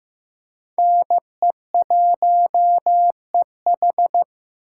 Morse code, heard as a keyed tone, about 15 wpm, 700 Hz. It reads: NE1EH
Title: USA Call Signs - 5 Characters - 4x Character Spacing - 15wpm